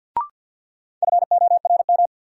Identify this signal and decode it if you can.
Morse code, heard as a keyed tone, about 52 wpm, 700 Hz. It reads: FORM
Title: Sentences from Top 1000 Words - Review Part 1 - New Word Drill - 50wpm